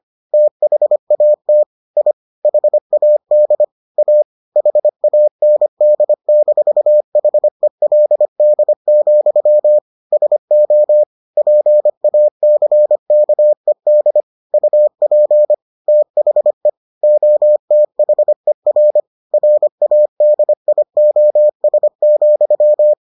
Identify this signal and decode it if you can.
Morse code, heard as a keyed tone, about 25 wpm, 600 Hz. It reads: THAT I HAD A HAND-HELD, SO PACKED UP THE OTHER RADIOS,